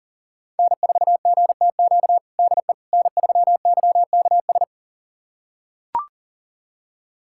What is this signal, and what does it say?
Morse code, heard as a keyed tone, about 40 wpm, 690 Hz. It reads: N4GTQ DE N3YKS
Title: USA Callsign DE Callsign - 5 Characters - 40wpm